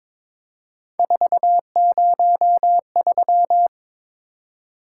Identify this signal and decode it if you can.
Morse code, heard as a keyed tone, about 22 wpm, 700 Hz. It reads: 403
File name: practice / 22wpm / Three Letter-Number Combinations - 22wpm